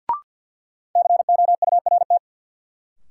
Morse code, heard as a keyed tone, about 53 wpm, 700 Hz. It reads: COURT
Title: Sentences from Top 1600 Words - New Word Drill - 50wpm